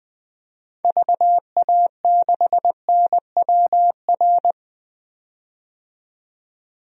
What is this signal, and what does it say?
Morse code, heard as a keyed tone, about 20 wpm, 700 Hz. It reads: VA6NWR
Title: DXCC-100 Call Signs - 6 Characters - 20wpm